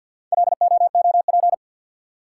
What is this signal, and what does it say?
Morse code, heard as a keyed tone, about 53 wpm, 700 Hz. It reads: LOOP
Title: Top 180 QSO Elements 50wpm